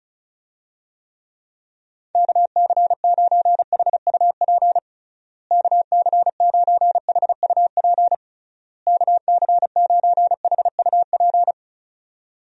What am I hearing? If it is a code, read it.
Morse code, heard as a keyed tone, about 36 wpm, 700 Hz. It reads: KC9HUP KC9HUP KC9HUP
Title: USA Call Signs - 6 Characters - Repeated 3x - 35wpm